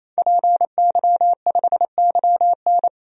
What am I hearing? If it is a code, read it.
Morse code, heard as a keyed tone, about 28 wpm, 700 Hz. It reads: PY5YN
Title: DXCC-100 Call Signs - 5 Characters - 28wpm